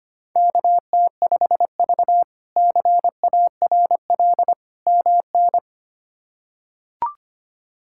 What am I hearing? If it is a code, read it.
Morse code, heard as a keyed tone, about 25 wpm, 700 Hz. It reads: KT5V CARL MN
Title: K1USN SST QSO Practice - 25wpm